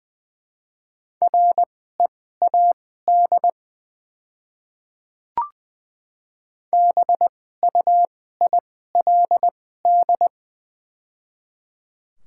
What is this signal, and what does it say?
Morse code, heard as a keyed tone, about 21 wpm, spaced as for 15 wpm, 700 Hz. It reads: READ BUILD
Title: 2x Character Spacing - Top 500 Words - 20wpm